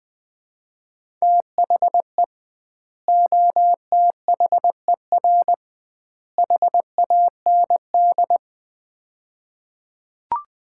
Morse code, heard as a keyed tone, about 21 wpm, 700 Hz. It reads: THE OTHER HAND
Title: 2x Word-Spacing - Sets of 3 Words - 20wpm